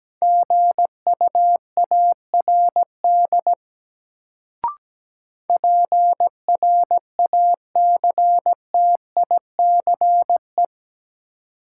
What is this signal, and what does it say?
Morse code, heard as a keyed tone, about 17 wpm, 700 Hz. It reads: GUARD PRACTICE
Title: Sentences from Top 700 Words - New Word Drill - 17wpm